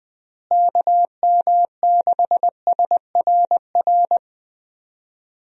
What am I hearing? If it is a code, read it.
Morse code, heard as a keyed tone, about 20 wpm, 700 Hz. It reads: KM6SRR
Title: USA Call Signs - 6 Characters - Repeated 3x - 20wpm